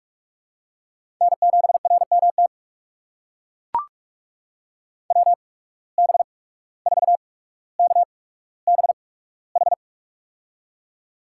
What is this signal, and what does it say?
Morse code, heard as a keyed tone, about 44 wpm, spaced as for 18 wpm, 700 Hz. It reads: N7RMT WB4KBH
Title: USA Call Signs - 8x Character Spacing - 45wpm